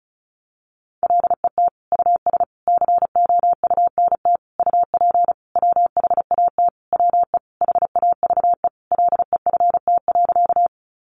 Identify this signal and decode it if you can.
Morse code, heard as a keyed tone, about 34 wpm, 700 Hz. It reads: LET US COUNT UP WHAT WE HAVE LEFT.
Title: Sentences from Top 1500 Words 35wpm